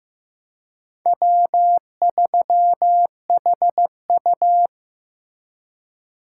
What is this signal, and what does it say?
Morse code, heard as a keyed tone, about 18 wpm, 700 Hz. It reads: W3HU
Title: Speed Racing - USA 1x2 Call Signs 15wpm